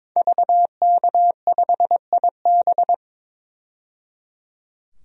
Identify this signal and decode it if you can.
Morse code, heard as a keyed tone, about 22 wpm, 700 Hz. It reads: VK5IB
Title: DXCC-100 Call Signs - 5 Characters - 22wpm